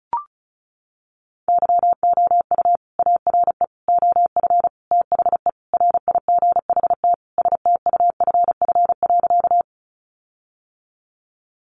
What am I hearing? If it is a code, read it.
Morse code, heard as a keyed tone, about 34 wpm, 700 Hz. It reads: YOU ARE OF THE RIGHT STUFF.
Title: Sentences from Top 1400 Words 35wpm